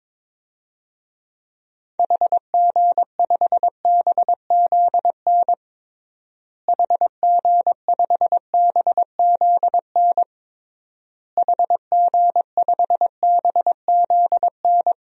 Morse code, heard as a keyed tone, about 21 wpm, 700 Hz. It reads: HG5BZN HG5BZN HG5BZN
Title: DXCC-100 Call Signs - 6 Characters - Repeated 3x - 22wpm